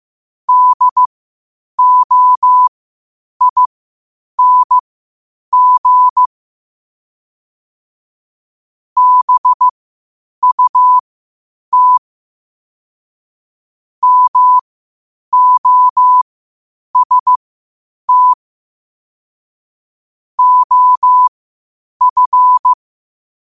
Morse code, heard as a keyed tone, about 15 wpm, spaced as for 8 wpm, 1000 Hz. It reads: DOING BUT MOST OF
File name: qotdfiles / 8wpm